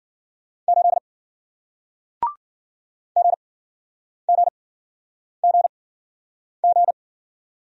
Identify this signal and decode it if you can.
Morse code, heard as a keyed tone, about 54 wpm, 700 Hz. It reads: 8 G G G G